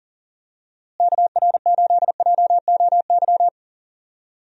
Morse code, heard as a keyed tone, about 39 wpm, 700 Hz. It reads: KR8JOY
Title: USA Call Signs - 6 Characters - Repeated 3x - 40wpm